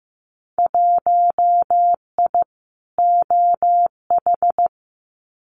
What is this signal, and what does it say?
Morse code, heard as a keyed tone, about 15 wpm, 700 Hz. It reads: 1I OH
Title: Winter Field Day Exchanges 15wpm